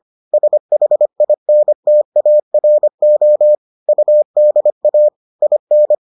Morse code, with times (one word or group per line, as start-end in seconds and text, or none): 0.3-3.5 SHINTARO
3.9-5.1 UDA
5.4-6.0 IN